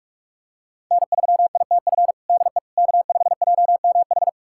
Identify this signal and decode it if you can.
Morse code, heard as a keyed tone, about 42 wpm, 700 Hz. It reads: N3ITF DE K5JMH
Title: USA Callsign DE Callsign - 5 Characters - 45wpm